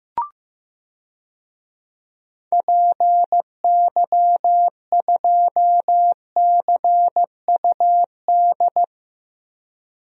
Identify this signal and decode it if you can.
Morse code, heard as a keyed tone, about 15 wpm, 700 Hz. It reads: PY2CUD
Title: DXCC-100 Call Signs - 6 Characters - 15wpm